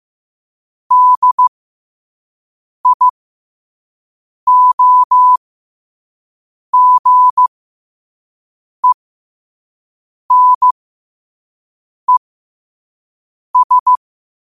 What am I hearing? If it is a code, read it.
Morse code, heard as a keyed tone, about 15 wpm, spaced as for 5 wpm, 1000 Hz. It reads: DIOGENES
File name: qotdfiles / 5wpm